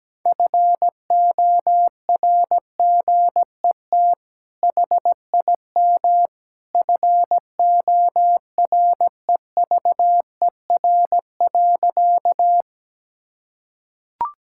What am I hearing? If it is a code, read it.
Morse code, heard as a keyed tone, about 17 wpm, 700 Hz. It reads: FORGET HIM FOREVER.
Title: Sentences from Top 1600 Words 17wpm